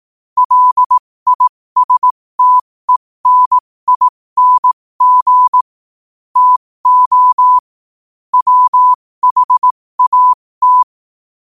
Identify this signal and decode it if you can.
Morse code, heard as a keyed tone, about 18 wpm, spaced as for 15 wpm, 1000 Hz. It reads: LISTENING TO WHAT
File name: qotdfiles / 15wpm